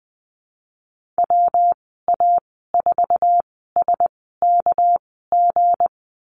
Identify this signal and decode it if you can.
Morse code, heard as a keyed tone, about 20 wpm, spaced as for 14 wpm, 700 Hz. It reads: WA4SKG
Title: 231 Lesson 37g - Letter x - Call Signs 20wpm